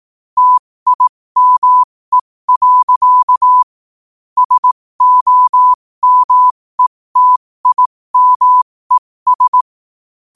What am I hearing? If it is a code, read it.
Morse code, heard as a keyed tone, about 18 wpm, spaced as for 15 wpm, 1000 Hz. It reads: TIME. SOMETIMES
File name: qotdfiles / 15wpm